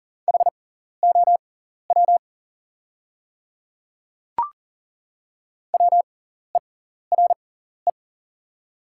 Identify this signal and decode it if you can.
Morse code, heard as a keyed tone, about 40 wpm, spaced as for 14 wpm, 700 Hz. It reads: HOW WERE
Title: 6x Character Spacing - Top 100 Words - 40wpm